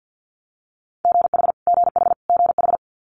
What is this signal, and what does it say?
Morse code, heard as a keyed tone, about 50 wpm, 700 Hz. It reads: GH GH GH
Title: Multi-letter Phonemes Repeated 3x 50wpm